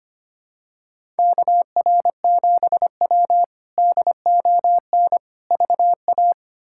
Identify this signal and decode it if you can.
Morse code, heard as a keyed tone, about 25 wpm, 700 Hz. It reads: KR7W DON VA
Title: K1USN SST QSO Practice - 25wpm